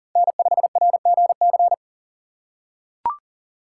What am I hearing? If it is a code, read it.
Morse code, heard as a keyed tone, about 40 wpm, 700 Hz. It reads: N5RGC
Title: USA Call Signs - 5 Characters - 4x Character Spacing - 40wpm